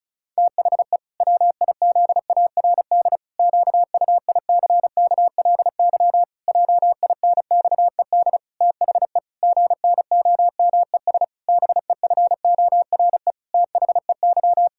Morse code, heard as a keyed tone, about 34 wpm, 700 Hz. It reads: THE WIZARD QUICKLY JINXED THE GNOMES BEFORE THEY
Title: Pangram Challenge v2 - 35wpm